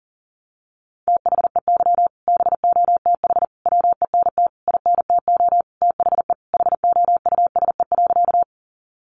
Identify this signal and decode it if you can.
Morse code, heard as a keyed tone, about 40 wpm, 700 Hz. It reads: THEY BOTH WENT INTO THE HOUSE.
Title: Sentences from Top 600 Words 40wpm